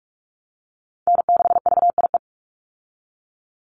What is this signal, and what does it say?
Morse code, heard as a keyed tone, about 46 wpm, 690 Hz. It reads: N6VIE
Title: USA Call Signs 45wpm